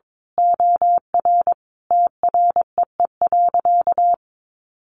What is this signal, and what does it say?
Morse code, heard as a keyed tone, about 22 wpm, 700 Hz. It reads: OR TREE.
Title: A Sound Of Thunder 22wpm